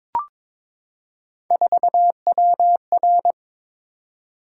Morse code, heard as a keyed tone, about 23 wpm, 700 Hz. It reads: 4WR
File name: practice / 22wpm / Three Letter-Number Combinations - 22wpm